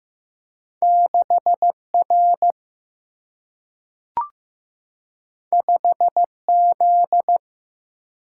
Morse code, heard as a keyed tone, about 15 wpm, 700 Hz. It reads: 6R 5Z
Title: Single-Letter Single-Number 15wpm